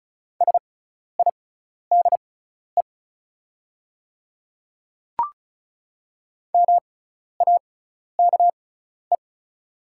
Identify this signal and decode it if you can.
Morse code, heard as a keyed tone, about 36 wpm, spaced as for 12 wpm, 700 Hz. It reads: SIDE MAKE
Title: 6x Character Spacing - Top 100 Words - 35wpm